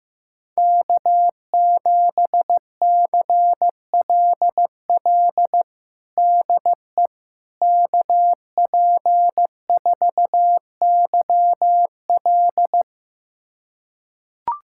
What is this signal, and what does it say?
Morse code, heard as a keyed tone, about 16 wpm, 700 Hz. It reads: K7CLL DE KP4YL
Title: USA Callsign DE Callsign - 5 Characters - 15wpm